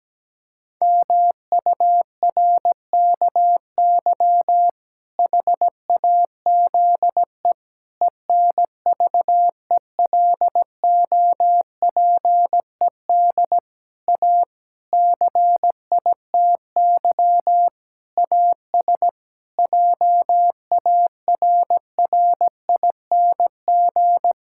Morse code, heard as a keyed tone, about 17 wpm, 700 Hz. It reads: MURKY HAZE ENVELOPED A CITY AS JARRING
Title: Pangram Challenge v2 - 17wpm